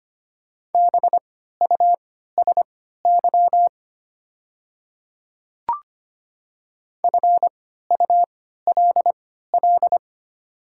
Morse code, heard as a keyed tone, about 25 wpm, spaced as for 14 wpm, 700 Hz. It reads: BUSY FULL
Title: 3x Character Spacing - Top 500 Words - 25wpm